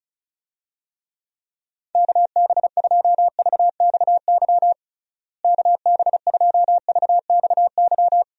Morse code, heard as a keyed tone, about 34 wpm, 700 Hz. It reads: KB2VXY KB2VXY
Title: USA Call Signs - 6 Characters - Repeated 3x - 35wpm